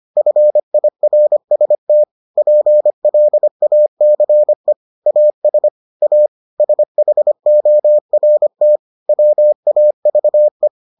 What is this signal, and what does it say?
Morse code, heard as a keyed tone, about 25 wpm, 600 Hz. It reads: FIRST PLACE AS A SHORT WAVE